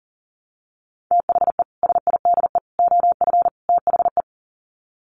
Morse code, heard as a keyed tone, about 40 wpm, 700 Hz. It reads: THE SIDE OF THE
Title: Sets of 4 Words 40wpm